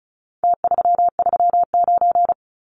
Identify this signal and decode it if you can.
Morse code, heard as a keyed tone, about 34 wpm, 700 Hz. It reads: T339
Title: Numbers - Contest Style - 4-Digit 35wpm